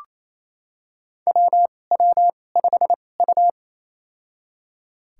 Morse code, heard as a keyed tone, about 27 wpm, spaced as for 20 wpm, 700 Hz. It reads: WW5U
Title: USA Call Signs - 2x Character Spacing - 28wpm